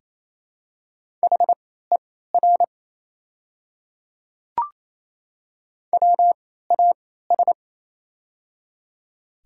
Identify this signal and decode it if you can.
Morse code, heard as a keyed tone, about 28 wpm, spaced as for 16 wpm, 700 Hz. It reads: HER WAS
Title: 3x Character Spacing - Top 100 Words - 28wpm